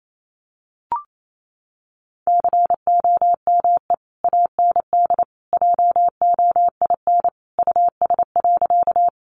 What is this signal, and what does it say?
Morse code, heard as a keyed tone, about 28 wpm, 700 Hz. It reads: COME AND JOIN US.
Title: Sentences from Top 1400 Words 28wpm